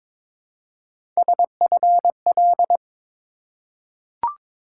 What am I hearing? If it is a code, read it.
Morse code, heard as a keyed tone, about 22 wpm, 700 Hz. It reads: SFL
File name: ARRL Sections v2 - 22wpm